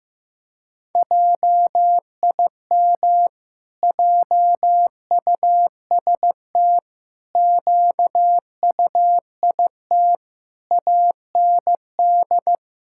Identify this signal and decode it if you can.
Morse code, heard as a keyed tone, about 15 wpm, 700 Hz. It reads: JIM JUST QUIT AND
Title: Pangram Challenge v2 - 15wpm